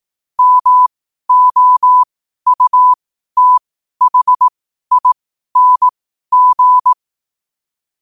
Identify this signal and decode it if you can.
Morse code, heard as a keyed tone, about 18 wpm, spaced as for 12 wpm, 1000 Hz. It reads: MOUTHING
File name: qotdfiles / 12wpm